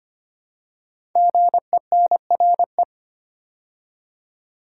Morse code, heard as a keyed tone, about 25 wpm, 700 Hz. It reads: GENRE
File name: 5 Letter Words - First 3 Letters Given - 25wpm